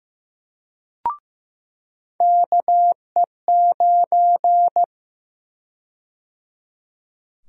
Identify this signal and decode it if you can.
Morse code, heard as a keyed tone, about 15 wpm, 700 Hz. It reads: KE9
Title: USA Call Sign Prefixes - 15wpm